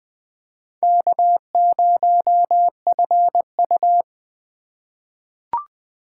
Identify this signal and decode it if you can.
Morse code, heard as a keyed tone, about 20 wpm, 700 Hz. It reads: K0FU
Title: USA Call Signs - 4x Character Spacing - 20wpm